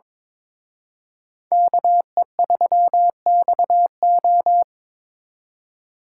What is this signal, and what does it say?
Morse code, heard as a keyed tone, about 26 wpm, 700 Hz. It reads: KE3XO
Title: Speed Racing - USA 2x2 Call Signs 22wpm